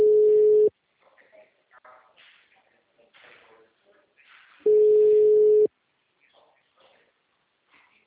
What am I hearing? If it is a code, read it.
Ringback tone (Europe, 425 Hz), with speech.